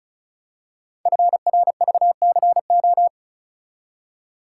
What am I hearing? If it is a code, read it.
Morse code, heard as a keyed tone, about 36 wpm, 700 Hz. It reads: FRVCO
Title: Code Groups - 5 Letters - 35wpm